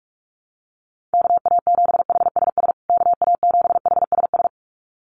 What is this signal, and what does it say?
Morse code, heard as a keyed tone, about 44 wpm, 700 Hz. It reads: KA7HSS KA7HSS
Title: USA Call Signs Repeated 2x 45wpm